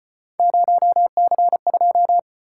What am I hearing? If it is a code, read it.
Morse code, heard as a keyed tone, about 34 wpm, 700 Hz. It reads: 0C2